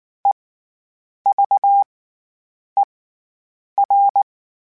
Morse code, heard as a keyed tone, about 19 wpm, spaced as for 8 wpm, 800 Hz. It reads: EVER
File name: Special INGs / Special INGs-mp3s / SlowToFaster ing @18